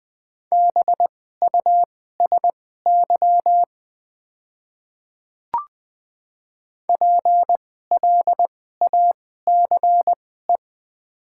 Morse code, heard as a keyed tone, about 20 wpm, spaced as for 15 wpm, 700 Hz. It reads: BUSY PLACE
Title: 2x Character Spacing - Top 500 Words - 20wpm